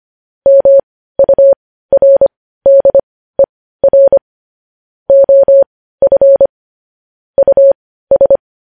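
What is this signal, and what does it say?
Morse code, heard as a keyed tone, about 25 wpm, spaced as for 15 wpm, 550 Hz. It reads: MURDER OF US
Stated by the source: Morse news headlines practice at 15 wpm.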